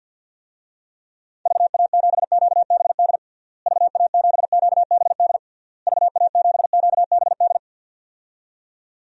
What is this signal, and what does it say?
Morse code, heard as a keyed tone, about 48 wpm, 700 Hz. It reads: VA7QBD VA7QBD VA7QBD
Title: DXCC-100 Call Signs - 6 Characters - Repeated 3x - 50wpm